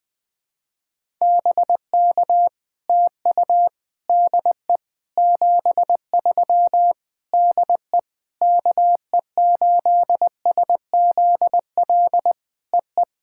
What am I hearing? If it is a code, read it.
Morse code, heard as a keyed tone, about 20 wpm, 700 Hz. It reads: BK TU DE 73 DE KE8SZL EE
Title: POTA QSO Practice - 20wpm